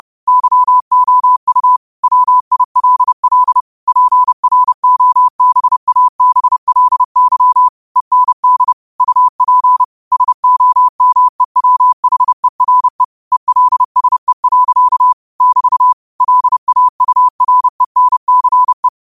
Morse code, heard as a keyed tone, about 30 wpm, 1000 Hz. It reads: YOU WILL PROBABLY END UP SOMEWHERE ELSE. <BT> LAURENCE
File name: qotdfiles / 30wpm